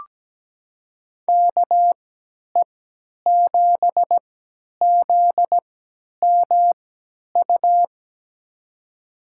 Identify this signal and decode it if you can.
Morse code, heard as a keyed tone, about 17 wpm, spaced as for 10 wpm, 700 Hz. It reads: KE7ZMU
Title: USA Call Signs - 3x Character Spacing - 17wpm